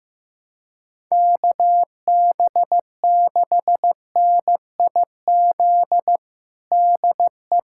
Morse code, heard as a keyed tone, about 15 wpm, 700 Hz. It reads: KB6NIZ DE
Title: USA Callsign DE Callsign - 6 Characters - 15wpm